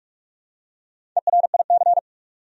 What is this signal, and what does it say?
Morse code, heard as a keyed tone, about 46 wpm, 700 Hz. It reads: ERIC
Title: USA Common Names - 45wpm